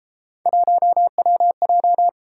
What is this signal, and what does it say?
Morse code, heard as a keyed tone, about 32 wpm, 700 Hz. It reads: 1WJ